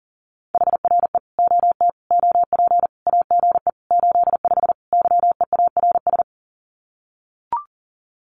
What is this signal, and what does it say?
Morse code, heard as a keyed tone, about 40 wpm, 700 Hz. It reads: HRE OT OP AGE 85 YEARS
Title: Common QSO Phrases 40wpm